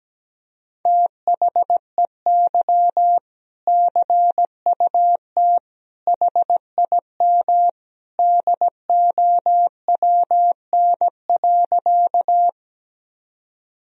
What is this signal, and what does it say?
Morse code, heard as a keyed tone, about 17 wpm, 700 Hz. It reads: THEY CUT HIM DOWN.
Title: Sentences from Top 1000 Words - Review Part 1 - 17wpm